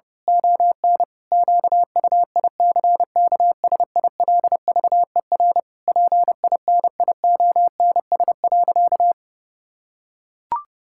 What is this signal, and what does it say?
Morse code, heard as a keyed tone, about 30 wpm, 700 Hz. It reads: ON QUICKSILVER PINIONS.
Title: Pangram Challenge v2 - 30wpm